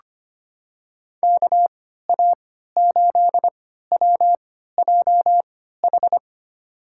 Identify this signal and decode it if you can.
Morse code, heard as a keyed tone, about 25 wpm, spaced as for 14 wpm, 700 Hz. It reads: KA8WJH
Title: USA Call Signs - 6 Characters - 3x Character Spacing - 25wpm